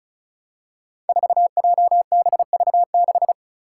Morse code, heard as a keyed tone, about 36 wpm, 700 Hz. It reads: 4JBV6
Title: Five Challenging Letter-Number Combinations 35wpm